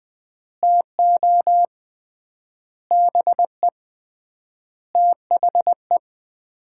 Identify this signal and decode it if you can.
Morse code, heard as a keyed tone, about 20 wpm, 700 Hz. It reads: TO BE THE
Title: 3x Word-Spacing - Sets of 5 Words - 20wpm